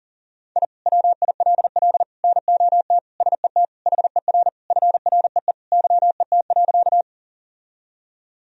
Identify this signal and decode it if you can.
Morse code, heard as a keyed tone, about 40 wpm, 700 Hz. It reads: I WILL NOT SET HER FREE YET.
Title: Sentences from Top 1000 Words - Review Part 1 - 40wpm